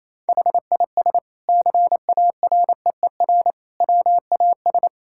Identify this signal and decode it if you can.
Morse code, heard as a keyed tone, about 28 wpm, 700 Hz. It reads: HIS CAREER WAS
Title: Sentences from Top 1000 Words - Review Part 6 - 28wpm